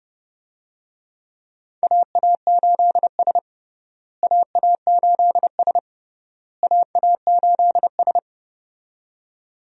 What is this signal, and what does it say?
Morse code, heard as a keyed tone, about 32 wpm, 700 Hz. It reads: AA8S AA8S AA8S
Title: USA Call Signs - 4 Characters - Repeated 3x - 30wpm